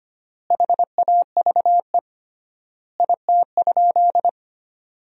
Morse code, heard as a keyed tone, about 25 wpm, 700 Hz. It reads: HAVE IT?
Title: 3x Word-Spacing - Sentences from Top 100 Words - 25wpm